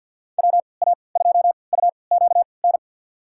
Morse code, heard as a keyed tone, about 48 wpm, spaced as for 28 wpm, 700 Hz. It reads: WA2UQN
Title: USA Call Signs - 6 Characters - 3x Character Spacing - 50wpm